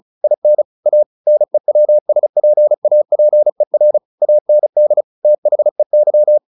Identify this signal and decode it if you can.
Morse code, heard as a keyed tone, about 34 wpm, 600 Hz. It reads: IN A NEWSPAPER AND THEY
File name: the old man and the sea / the old man and the sea 1 35wpm